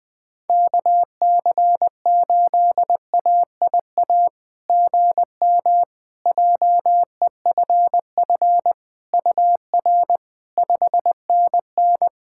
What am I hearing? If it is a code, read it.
Morse code, heard as a keyed tone, about 20 wpm, 700 Hz. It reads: KC8AIA GM JEFF UR 5NN
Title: SOTA QSO Practice 20wpm